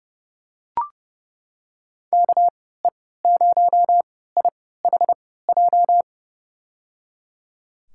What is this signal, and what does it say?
Morse code, heard as a keyed tone, about 30 wpm, spaced as for 17 wpm, 700 Hz. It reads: KE0IHJ
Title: USA Call Signs - 6 Characters - 3x Character Spacing - 30wpm